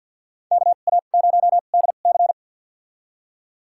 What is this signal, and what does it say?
Morse code, heard as a keyed tone, about 52 wpm, spaced as for 37 wpm, 700 Hz. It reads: KA0DC
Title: USA Call Signs - 2x Character Spacing - 50wpm